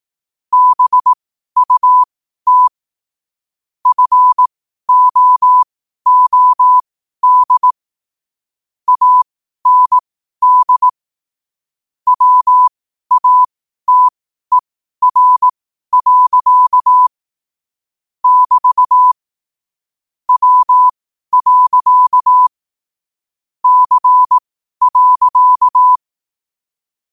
Morse code, heard as a keyed tone, about 18 wpm, spaced as for 12 wpm, 1000 Hz. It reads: BUT FOOD AND WATER. <BT> W. C.
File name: qotdfiles / 12wpm